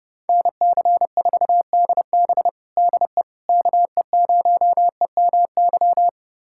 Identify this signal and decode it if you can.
Morse code, heard as a keyed tone, about 32 wpm, 700 Hz. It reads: NC4DB DE KE0EMY
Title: USA Callsign DE Callsign 30wpm